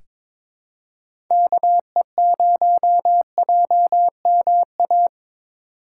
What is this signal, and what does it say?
Morse code, heard as a keyed tone, about 22 wpm, 700 Hz. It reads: KE0JMA
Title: USA Call Signs - 6 Characters - 4x Character Spacing - 22wpm